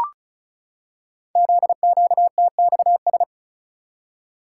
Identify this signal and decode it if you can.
Morse code, heard as a keyed tone, about 36 wpm, 700 Hz. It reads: ZQTXS
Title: Code Groups - 5 Letters - 35wpm